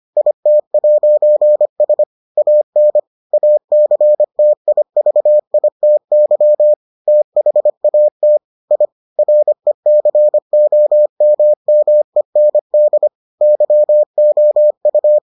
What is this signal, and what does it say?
Morse code, heard as a keyed tone, about 25 wpm, 600 Hz. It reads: IT'S AN ACTIVITY THAT I RECOMMEND YOU